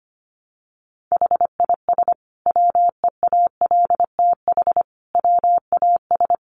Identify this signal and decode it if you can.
Morse code, heard as a keyed tone, about 25 wpm, 700 Hz. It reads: HIS WEALTH WAS
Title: Sentences from Top 1200 Words 25wpm